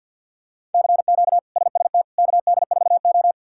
Morse code, heard as a keyed tone, about 47 wpm, 700 Hz. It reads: CQ SST KD4O
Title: K1USN SST QSO Practice - 50wpm